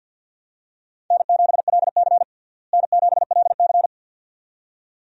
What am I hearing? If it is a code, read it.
Morse code, heard as a keyed tone, about 52 wpm, 700 Hz. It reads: N7LC N7LC
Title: USA Call Signs - 4 Characters - Repeated 2x - 50wpm